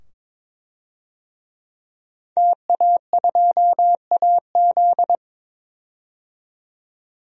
Morse code, heard as a keyed tone, about 22 wpm, 700 Hz. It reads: TA2AZ
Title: DXCC-100 Call Signs - 5 Characters - 22wpm